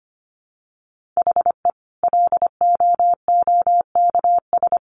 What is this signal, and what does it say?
Morse code, heard as a keyed tone, about 25 wpm, 700 Hz. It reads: HE LOOKS
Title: Additional Easy Phrases 25wpm